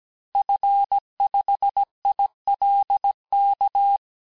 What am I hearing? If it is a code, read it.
Morse code, heard as a keyed tone, about 17 wpm, 800 Hz. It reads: F5ILK
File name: cours-du-lundi files / cours-1587-FIN-de-QSO-vitesse-16MPM